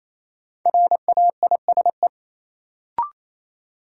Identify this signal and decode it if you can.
Morse code, heard as a keyed tone, about 28 wpm, 700 Hz. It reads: RAISE
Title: Sentences from Top 1400 Words - New Word Drill Alt - 28wpm